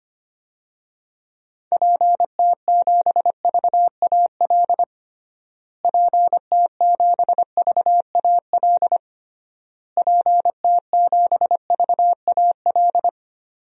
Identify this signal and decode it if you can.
Morse code, heard as a keyed tone, about 25 wpm, 700 Hz. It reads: PT7VAL PT7VAL PT7VAL
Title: DXCC-100 Call Signs - 6 Characters - Repeated 3x - 25wpm